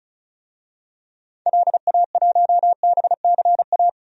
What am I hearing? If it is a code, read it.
Morse code, heard as a keyed tone, about 36 wpm, 700 Hz. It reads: LA1BCA
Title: DXCC-100 Call Signs - 6 Characters - 35wpm